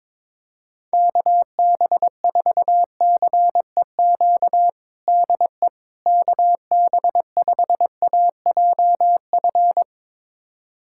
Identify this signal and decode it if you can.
Morse code, heard as a keyed tone, about 21 wpm, 700 Hz. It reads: KB4CEQ DE KB5AJF
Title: USA Callsign DE Callsign 22wpm